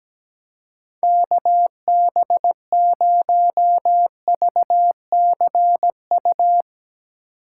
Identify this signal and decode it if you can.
Morse code, heard as a keyed tone, about 18 wpm, 700 Hz. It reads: KB0VCU
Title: Speed Racing - USA 2x3 Call Signs 17wpm